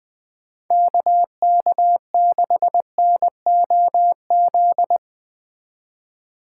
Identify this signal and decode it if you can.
Morse code, heard as a keyed tone, about 20 wpm, 700 Hz. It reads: KK6NOZ
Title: Speed Racing - USA 2x3 Call Signs 17wpm